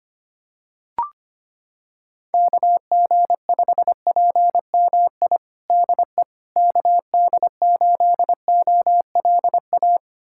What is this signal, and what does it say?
Morse code, heard as a keyed tone, about 25 wpm, 700 Hz. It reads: KG5PMI DE KD8OLA
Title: USA Callsign DE Callsign - 6 Characters - 25wpm